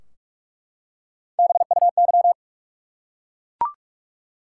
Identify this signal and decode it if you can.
Morse code, heard as a keyed tone, about 44 wpm, 700 Hz. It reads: BUY
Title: Three Letter Words - 45wpm